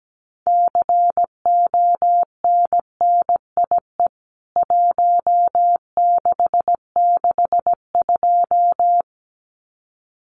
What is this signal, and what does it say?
Morse code, heard as a keyed tone, about 17 wpm, 700 Hz. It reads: CONNIE 1662
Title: CWT Exchange 17wpm